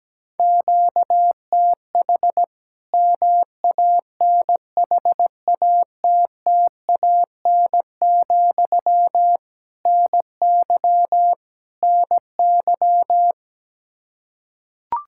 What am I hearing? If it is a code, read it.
Morse code, heard as a keyed tone, about 17 wpm, 700 Hz. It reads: QTH MANHATTAN, NY NY
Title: Practice QSO - Exchange 1 - Beginning - 17wpm